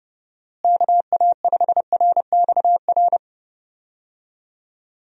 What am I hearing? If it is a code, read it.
Morse code, heard as a keyed tone, about 30 wpm, 700 Hz. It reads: KA5RXR
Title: USA Call Signs - 6 Characters - Repeated 3x - 30wpm